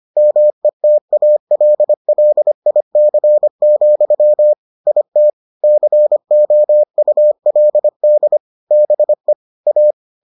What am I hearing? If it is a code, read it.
Morse code, heard as a keyed tone, about 25 wpm, 600 Hz. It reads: METALLIC, IT COULD BE A